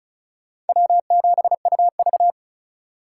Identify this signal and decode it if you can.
Morse code, heard as a keyed tone, about 48 wpm, spaced as for 42 wpm, 700 Hz. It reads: W7UV
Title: Speed Racing - USA 1x2 Call Signs 35wpm